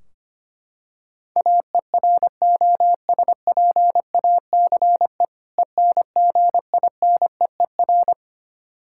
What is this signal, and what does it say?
Morse code, heard as a keyed tone, about 25 wpm, 700 Hz. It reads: AEROSPACE ENGINEER
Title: K1USN SST QSO Practice 25wpm